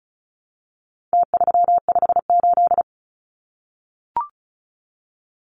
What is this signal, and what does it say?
Morse code, heard as a keyed tone, about 34 wpm, 700 Hz. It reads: T358
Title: Numbers - Contest Style - 4-Digit 35wpm